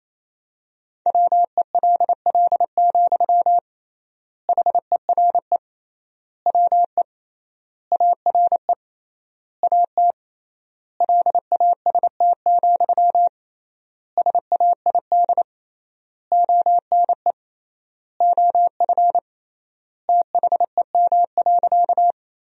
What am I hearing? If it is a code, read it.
Morse code, heard as a keyed tone, about 28 wpm, 700 Hz. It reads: WELL, HERE WE ARE AT LAST, SAID ONE OF THEM.
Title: 3x Word-Spacing - Sentences from Top 300 Words - 28wpm